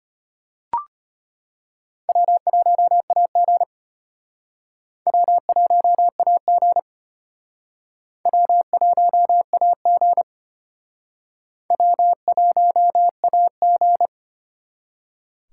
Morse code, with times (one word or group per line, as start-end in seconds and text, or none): 2.1-3.6 W1AG
5.1-6.8 W1AG
8.3-10.2 W1AG
11.7-14.1 W1AG